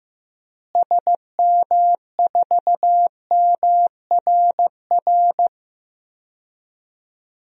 Morse code, heard as a keyed tone, about 15 wpm, 700 Hz. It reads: SM4MRR
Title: DXCC-100 Call Signs - 6 Characters - Repeated 2x - 15wpm